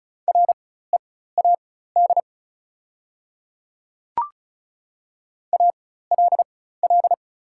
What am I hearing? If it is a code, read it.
Morse code, heard as a keyed tone, about 36 wpm, spaced as for 16 wpm, 700 Hz. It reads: READ ALL
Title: 4x Character Spacing - Top 300 Words - 35wpm